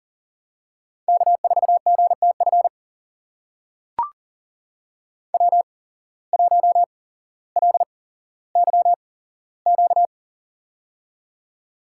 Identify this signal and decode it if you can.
Morse code, heard as a keyed tone, about 42 wpm, spaced as for 17 wpm, 700 Hz. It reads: K4GTF W1LYQ
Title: USA Call Signs - 5 Characters - 8x Character Spacing - 40wpm